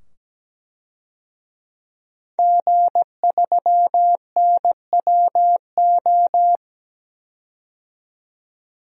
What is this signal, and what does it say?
Morse code, heard as a keyed tone, about 17 wpm, 700 Hz. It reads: G3NWO
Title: DXCC-100 Call Signs - 5 Characters - 17wpm